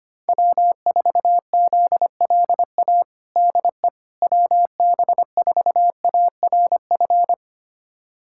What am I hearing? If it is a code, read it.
Morse code, heard as a keyed tone, about 25 wpm, 700 Hz. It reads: W4ZLA DE WB4ARF
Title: USA Callsign DE Callsign 25wpm